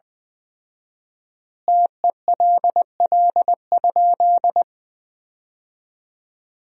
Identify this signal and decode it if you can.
Morse code, heard as a keyed tone, about 21 wpm, 700 Hz. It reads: TELL?
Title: 4x Word-Spacing - Sentences from Top 200 Words - 20wpm